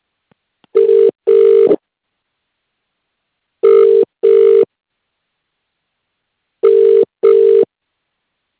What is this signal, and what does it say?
Ringback tone (UK), 400 + 450 Hz.